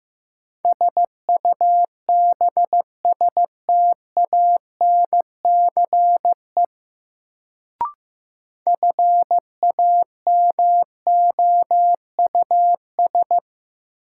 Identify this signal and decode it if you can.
Morse code, heard as a keyed tone, about 16 wpm, 700 Hz. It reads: SUBSTANCE FAMOUS
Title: Sentences from Top 1500 Words - New Word Drill - 15wpm